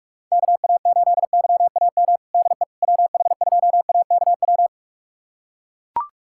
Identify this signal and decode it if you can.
Morse code, heard as a keyed tone, about 42 wpm, 700 Hz. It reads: KA8YAM DE WH2AKW
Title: USA Callsign DE Callsign - 6 Characters - 45wpm